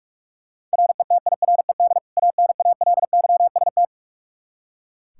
Morse code, heard as a keyed tone, about 44 wpm, 700 Hz. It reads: RETIRED ANALYST
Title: Practice QSO - Exchange 3 - More Info - 45wpm